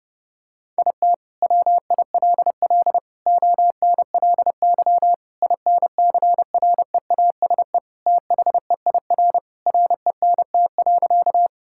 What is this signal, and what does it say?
Morse code, heard as a keyed tone, about 30 wpm, 700 Hz. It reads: IT WILL ONLY INCREASE THEIR RENT.
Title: Sentences from Top 1000 Words - Review Part 5 - 30wpm